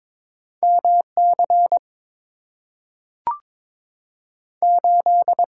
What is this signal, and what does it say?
Morse code, heard as a keyed tone, about 22 wpm, 700 Hz. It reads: MC 8
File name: practice / 22wpm / Two Letter-Number Combinations - 22wpm